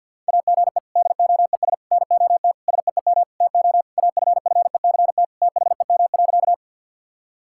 Morse code, heard as a keyed tone, about 50 wpm, 700 Hz. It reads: AGE DOES NOT SEEM TO AFFECT THEM.
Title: Sentences from Top 1000 Words - Review Part 6 - 50wpm